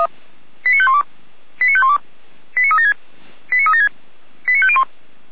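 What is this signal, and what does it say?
Five-tone selective call (ZVEI), heard as quick runs of five tones, tones of about 70 ms. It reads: 89512 89512 89376 89376 89501
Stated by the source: ZVEI five-tone call 89115 received on an SDR.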